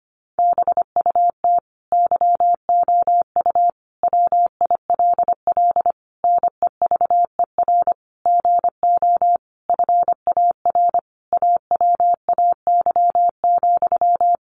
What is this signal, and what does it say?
Morse code, heard as a keyed tone, about 25 wpm, 700 Hz. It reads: BUT YOU WILL NEVER GO FAR AWAY,
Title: Sentences from Top 1500 Words 25wpm